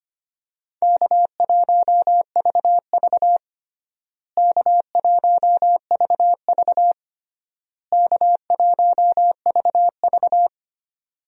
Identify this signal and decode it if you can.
Morse code, heard as a keyed tone, about 25 wpm, 700 Hz. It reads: K1VV K1VV K1VV
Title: USA Call Signs - 4 Characters - Repeated 3x - 25wpm